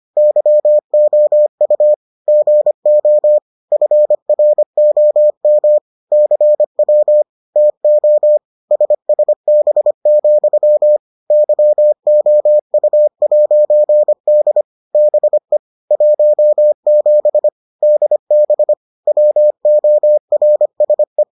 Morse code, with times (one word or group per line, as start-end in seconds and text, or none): 0.2-1.9 YOU
2.3-3.4 GO
3.7-5.8 FROM
6.1-7.2 CW
7.6-8.4 TO
8.7-11.0 SSB,
11.3-14.6 YOU'D
14.9-15.6 BE
15.9-17.5 17
17.8-18.7 DB
19.1-21.2 WORSE